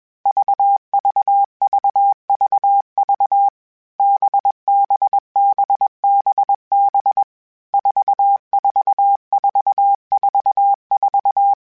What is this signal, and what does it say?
Morse code, heard as a keyed tone, about 21 wpm, 800 Hz. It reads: VVVVV BBBBB 44444